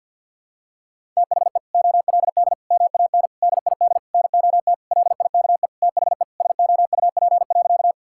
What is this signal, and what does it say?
Morse code, heard as a keyed tone, about 52 wpm, 700 Hz. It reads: THE OLD MAN DID NOT LIKE THE SOUP.
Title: Sentences from Top 1000 Words - Review Part 6 - 50wpm